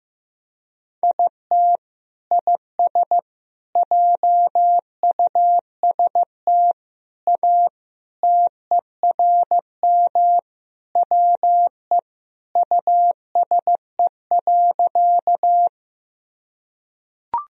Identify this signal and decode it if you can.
Morse code, heard as a keyed tone, about 15 wpm, 700 Hz. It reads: IT IS JUST A TERM WE USE.
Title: Sentences from Top 1000 Words - Review Part 4 - 15wpm